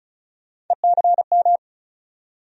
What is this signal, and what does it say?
Morse code, heard as a keyed tone, about 36 wpm, 700 Hz. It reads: ECM
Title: USA Call Sign Postfixes - 3 Letters 35wpm